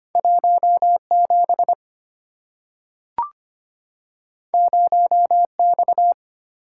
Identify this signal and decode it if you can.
Morse code, heard as a keyed tone, about 25 wpm, 690 Hz. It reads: 17 0X
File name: Two Letter-Number Combinations - 25wpm